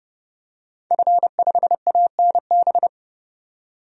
Morse code, heard as a keyed tone, about 30 wpm, 700 Hz. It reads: F5ANB
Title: DXCC-100 Call Signs - 5 Characters - 30wpm